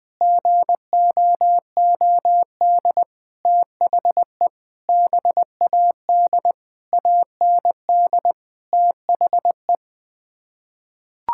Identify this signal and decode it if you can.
Morse code, heard as a keyed tone, about 20 wpm, 700 Hz. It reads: GOOD THE BAD AND THE
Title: Sets of 6 Words - 20wpm